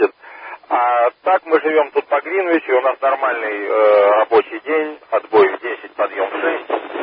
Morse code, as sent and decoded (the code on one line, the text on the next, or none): none